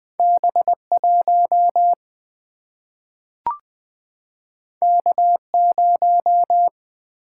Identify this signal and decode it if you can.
Morse code, heard as a keyed tone, about 20 wpm, 700 Hz. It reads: B 1 K 0
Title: Single-Letter Single-Number 20wpm